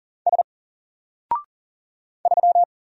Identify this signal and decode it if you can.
Morse code, heard as a keyed tone, about 40 wpm, 700 Hz. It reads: S 3
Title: Single Letter-Number Alt 40wpm